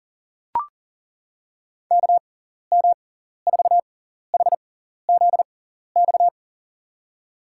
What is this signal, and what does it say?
Morse code, heard as a keyed tone, about 39 wpm, spaced as for 14 wpm, 700 Hz. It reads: KM4HZX
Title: USA Call Signs - 6 Characters - 6x Character Spacing - 40wpm